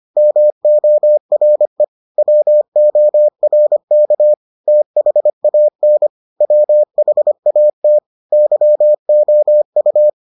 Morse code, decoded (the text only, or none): MORE WORK THAN WHAT YOU